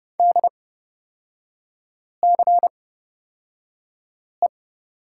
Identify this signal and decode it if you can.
Morse code, heard as a keyed tone, about 31 wpm, spaced as for 5 wpm, 700 Hz. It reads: DCE
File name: Lesson 36e - BT - Three Characters 30wpm